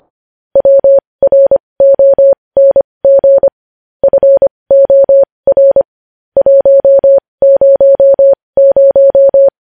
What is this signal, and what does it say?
Morse code, heard as a keyed tone, about 25 wpm, spaced as for 20 wpm, 550 Hz. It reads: WRONG FOR 100
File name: podcast / News headlines-1x-25x20